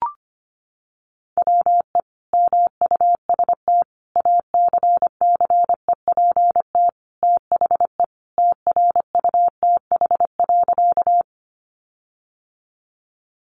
Morse code, heard as a keyed tone, about 25 wpm, 700 Hz. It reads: WE MUST ACCEPT THE TRUTH.
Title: Sentences from Top 1300 Words 25wpm